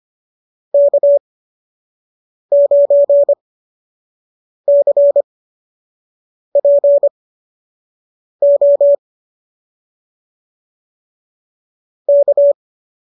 Morse code, heard as a keyed tone, about 25 wpm, spaced as for 6 wpm, 590 Hz. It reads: K9CPO K